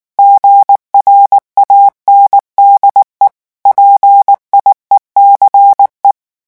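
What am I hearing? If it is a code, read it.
Morse code, heard as a keyed tone, about 19 wpm, 800 Hz. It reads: GRANDE PIECE
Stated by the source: French Morse revision 8 drill of random five-letter groups at 18 words per minute.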